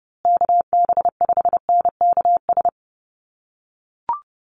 Morse code, heard as a keyed tone, about 30 wpm, 700 Hz. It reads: KB5NKS
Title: USA Call Signs Repeated 3x 30wpm